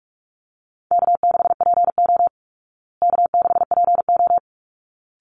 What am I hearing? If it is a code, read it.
Morse code, heard as a keyed tone, about 44 wpm, 700 Hz. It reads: K6PO K6PO
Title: Easy Call Signs Repeated 2x - 45wpm